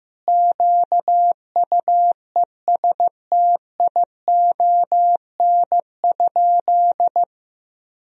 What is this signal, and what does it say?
Morse code, heard as a keyed tone, about 15 wpm, 700 Hz. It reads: QUESTION?
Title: Sentences from Top 1000 Words - Review Part 4 - 15wpm